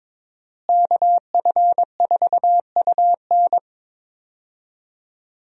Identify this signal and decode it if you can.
Morse code, heard as a keyed tone, about 22 wpm, 700 Hz. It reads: KF4UN
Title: USA Call Signs - 5 Characters - Repeated 3x - 22wpm